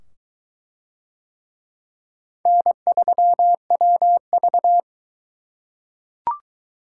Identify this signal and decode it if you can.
Morse code, heard as a keyed tone, about 21 wpm, 700 Hz. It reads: N3WV